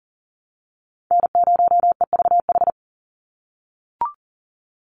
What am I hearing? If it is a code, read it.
Morse code, heard as a keyed tone, about 40 wpm, 700 Hz. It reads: N0EVH
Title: NA SOTA Chaser Callsigns 40wpm